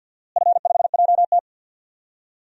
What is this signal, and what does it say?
Morse code, heard as a keyed tone, about 50 wpm, 700 Hz. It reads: F5JT
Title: DXCC-100 Call Signs - 4 Characters - Repeated 3x - 50wpm